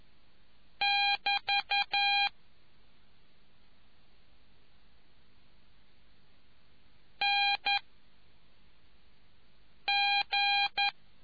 Morse code, heard as a keyed tone, about 11 wpm, 2400 Hz. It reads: <BT> N G